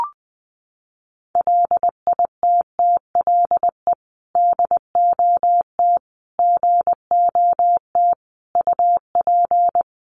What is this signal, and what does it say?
Morse code, heard as a keyed tone, about 20 wpm, 700 Hz. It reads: LITTLE DOT GOT UP
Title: Sentences from Top 1500 Words 20wpm